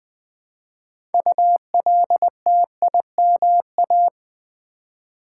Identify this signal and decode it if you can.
Morse code, heard as a keyed tone, about 20 wpm, 700 Hz. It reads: ULTIMA
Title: Build A Word - 3 Letters At A Time - 8-Letter Words - 20wpm